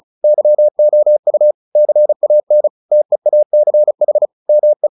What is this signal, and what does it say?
Morse code, heard as a keyed tone, about 34 wpm, 600 Hz. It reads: YOU CAN TEACH ME
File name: the old man and the sea 2 35wpm